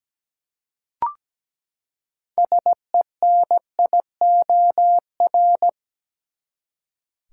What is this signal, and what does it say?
Morse code, heard as a keyed tone, about 17 wpm, 700 Hz. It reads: SENIOR
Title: Six Letter Words - 17wpm